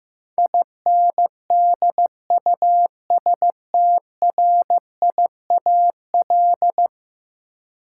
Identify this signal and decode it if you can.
Morse code, heard as a keyed tone, about 15 wpm, 700 Hz. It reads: INDUSTRIAL
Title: Sets of 5 Words - New Word Drill - 15wpm